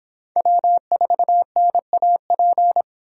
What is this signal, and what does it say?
Morse code, heard as a keyed tone, about 24 wpm, 700 Hz. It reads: W4NAP